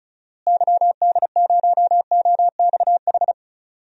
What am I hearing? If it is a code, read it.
Morse code, heard as a keyed tone, about 34 wpm, 700 Hz. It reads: YD0OXH
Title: DXCC-100 Call Signs - 6 Characters - Repeated 2x - 35wpm